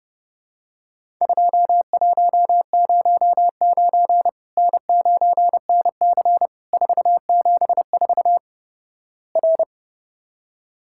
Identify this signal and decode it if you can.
Morse code, heard as a keyed tone, about 30 wpm, 700 Hz. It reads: 2109 N9NC 474 R
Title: Worked All Europe DX Contest - QTC Exchange - 30wpm